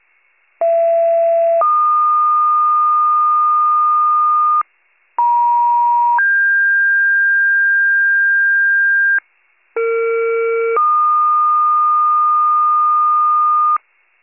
Two-tone page, heard as a one-second tone, then a three-second tone, about 670 then 1220 Hz, about 950 then 1600 Hz, about 460 then 1220 Hz. It is heard from 0.6 to 13.8 s.